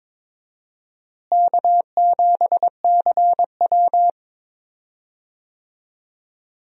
Morse code, heard as a keyed tone, about 22 wpm, 700 Hz. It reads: K7CW
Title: USA Call Signs - 4 Characters - Repeated 2x - 22wpm